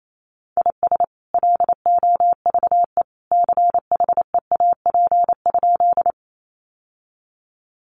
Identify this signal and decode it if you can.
Morse code, heard as a keyed tone, about 28 wpm, 700 Hz. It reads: IS LOVE CHEAP?
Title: Sentences from Top 1200 Words 28wpm